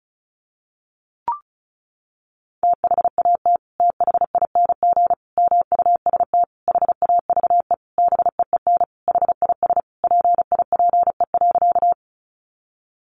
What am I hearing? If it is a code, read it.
Morse code, heard as a keyed tone, about 34 wpm, 700 Hz. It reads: THAT THING MUST HAVE BEEN HIS PIPE.
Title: Sentences from Top 1500 Words 35wpm